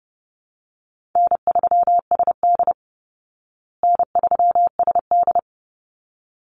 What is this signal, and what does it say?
Morse code, heard as a keyed tone, about 30 wpm, 700 Hz. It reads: N3SD N3SD
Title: Easy Call Signs Repeated 2x - 30wpm